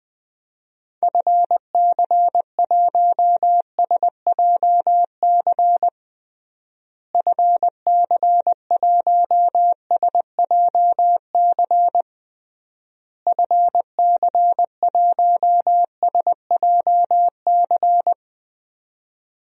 Morse code, heard as a keyed tone, about 21 wpm, 700 Hz. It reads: FC1SJC FC1SJC FC1SJC
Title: DXCC-100 Call Signs - 6 Characters - Repeated 3x - 20wpm